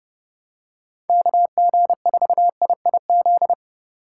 Morse code, heard as a keyed tone, about 30 wpm, 700 Hz. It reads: KG4IIZ
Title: USA Call Signs - 6 Characters - 2x Character Spacing - 30wpm